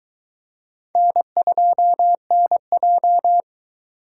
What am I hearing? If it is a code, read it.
Morse code, heard as a keyed tone, about 22 wpm, 700 Hz. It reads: N2NJ